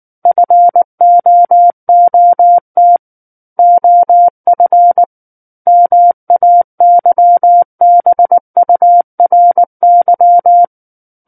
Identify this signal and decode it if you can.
Morse code, heard as a keyed tone, about 19 wpm, 700 Hz. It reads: FOOT OF MAYBURY
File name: Book One / 10-Chapter 10 - 19 WPM